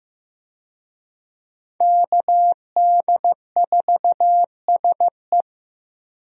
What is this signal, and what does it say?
Morse code, heard as a keyed tone, about 15 wpm, 700 Hz. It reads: KD4SE
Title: USA Call Signs - 5 Characters - Repeated 2x - 15wpm